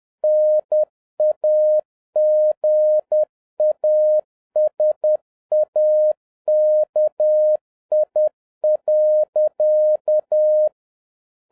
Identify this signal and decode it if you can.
Morse code, heard as a keyed tone, about 10 wpm, 610 Hz. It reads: NAGASAKI.